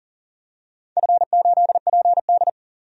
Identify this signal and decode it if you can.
Morse code, heard as a keyed tone, about 40 wpm, 700 Hz. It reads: F8PD
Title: DXCC-100 Call Signs - 4 Characters - 40wpm